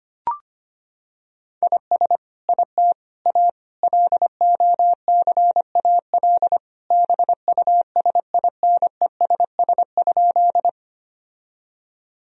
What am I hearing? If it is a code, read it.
Morse code, heard as a keyed tone, about 25 wpm, 700 Hz. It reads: IS IT A LOCAL BUSINESS?
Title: Sentences from Top 1000 Words - Review Part 3 - 25wpm